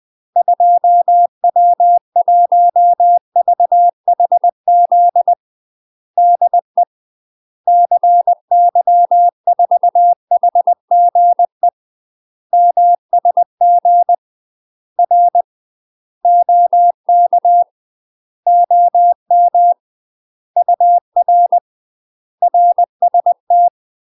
Morse code, heard as a keyed tone, about 20 wpm, 700 Hz. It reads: 2W1VHZ DE CY4HGE MSG R OK OM UR RST
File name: UK MASTER QSOs TRAINING / 20WPM / UK MASTER QSOs 0040